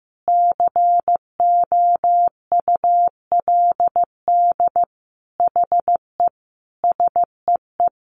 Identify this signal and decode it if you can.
Morse code, heard as a keyed tone, about 15 wpm, 700 Hz. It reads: COULD HE SEE
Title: Sentences from Top 1300 Words 15wpm